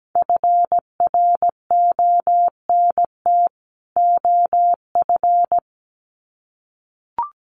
Morse code, heard as a keyed tone, about 17 wpm, 700 Hz. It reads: FRONT OF
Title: Sets of 2 Words 17wpm